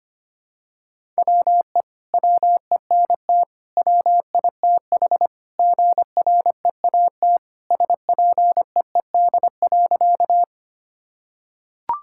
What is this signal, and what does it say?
Morse code, heard as a keyed tone, about 25 wpm, 700 Hz. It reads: WE WENT WITH GREAT SPEED.
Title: Sentences from Top 1000 Words - Review Part 5 - 25wpm